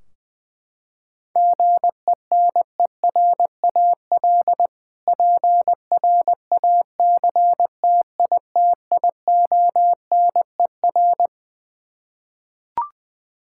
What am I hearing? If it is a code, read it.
Morse code, heard as a keyed tone, about 20 wpm, 700 Hz. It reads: GENERAL PRACTITIONER
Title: Practice QSO - Exchange 3 - More Info - 20wpm